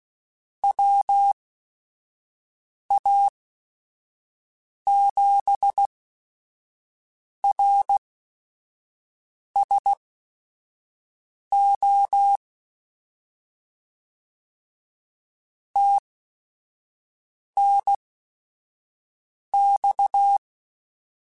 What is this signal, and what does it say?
Morse code, heard as a keyed tone, about 16 wpm, spaced as for 5 wpm, 800 Hz. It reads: WA7RSO TNX
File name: QSOs / QSOs-mp3s / kc7sac@5